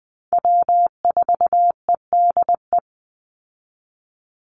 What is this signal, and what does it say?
Morse code, heard as a keyed tone, about 20 wpm, 700 Hz. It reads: W4EDE
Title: Call Signs Repeated 2x - 20wpm